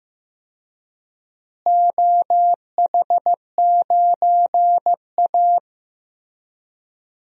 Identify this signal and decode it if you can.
Morse code, heard as a keyed tone, about 15 wpm, 700 Hz. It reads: OH9A
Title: DXCC-100 Call Signs - 4 Characters - Repeated 2x - 15wpm